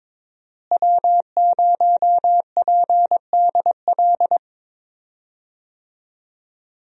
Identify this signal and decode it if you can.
Morse code, heard as a keyed tone, about 22 wpm, 700 Hz. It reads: W0PDL
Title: USA Call Signs - 5 Characters - Repeated 3x - 22wpm